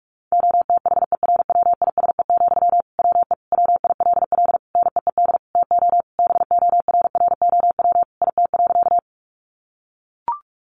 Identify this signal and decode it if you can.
Morse code, heard as a keyed tone, about 44 wpm, 700 Hz. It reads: OTHERWISE, WE WILL NEED TO BORROW IT.
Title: Sentences from Top 1500 Words 45wpm